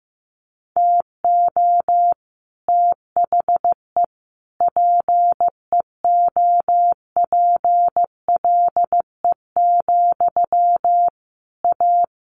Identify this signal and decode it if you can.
Morse code, heard as a keyed tone, about 15 wpm, 700 Hz. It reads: TO THE PEOPLE, A
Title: Sentences from Top 1500 Words 15wpm